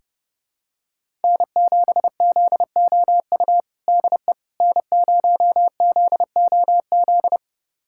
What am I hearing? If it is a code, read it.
Morse code, heard as a keyed tone, about 32 wpm, 700 Hz. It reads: N7ZOU DE N0ZOZ
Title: USA Callsign DE Callsign - 5 Characters - 30wpm